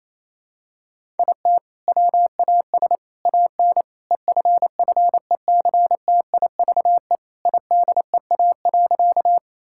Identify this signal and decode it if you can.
Morse code, heard as a keyed tone, about 28 wpm, 700 Hz. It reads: IT WAS AN EFFECTIVE IDEA.
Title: Sentences from Top 1000 Words - Review Part 6 - 28wpm